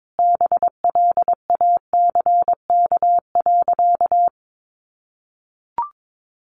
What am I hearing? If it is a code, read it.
Morse code, heard as a keyed tone, about 22 wpm, 700 Hz. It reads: BLACK.
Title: Easy Phrases 22wpm